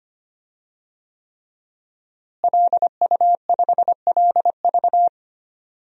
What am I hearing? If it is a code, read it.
Morse code, heard as a keyed tone, about 25 wpm, 700 Hz. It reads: LU5LV
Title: DXCC-100 Call Signs - 5 Characters - Repeated 3x - 25wpm